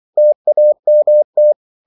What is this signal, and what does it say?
Morse code, heard as a keyed tone, about 24 wpm, 600 Hz. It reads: TAMT
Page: Korte danske ord 24wpm